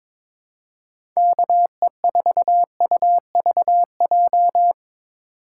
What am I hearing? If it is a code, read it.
Morse code, heard as a keyed tone, about 22 wpm, 700 Hz. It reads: KE4UVJ
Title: USA Call Signs - 4x Character Spacing - 22wpm